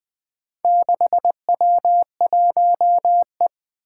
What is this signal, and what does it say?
Morse code, heard as a keyed tone, about 20 wpm, 700 Hz. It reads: 6W1E
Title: Four Challenging Letter-Number Combinations 20wpm